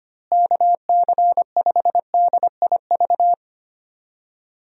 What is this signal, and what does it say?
Morse code, heard as a keyed tone, about 25 wpm, 700 Hz. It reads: KC5DIV
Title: USA Call Signs - 6 Characters - Repeated 2x - 25wpm